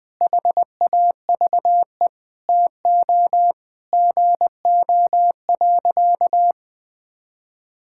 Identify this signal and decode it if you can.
Morse code, heard as a keyed tone, about 20 wpm, 700 Hz. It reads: HAVE TO GO.
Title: Sentences with 4-Word Phrases - 20wpm